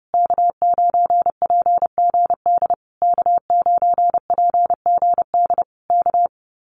Morse code, heard as a keyed tone, about 30 wpm, 700 Hz. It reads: K9PGD K9PGD K
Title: Common QSO Phrases Encore 30wpm